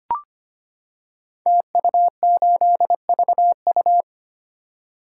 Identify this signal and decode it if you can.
Morse code, heard as a keyed tone, about 25 wpm, 700 Hz. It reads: TU8VU
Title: Code Groups - 5 Letter:Numbers - 25wpm